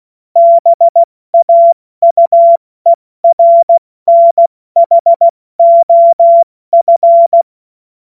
Morse code, heard as a keyed tone, about 16 wpm, spaced as for 14 wpm, 680 Hz. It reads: BAUERNHOF
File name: Morsekurs / Witze / Witz 07. März 2023 70 F80